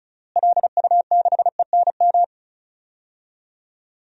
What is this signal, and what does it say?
Morse code, heard as a keyed tone, about 36 wpm, 700 Hz. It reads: LU6ENM
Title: DXCC-100 Call Signs - 6 Characters - Repeated 3x - 35wpm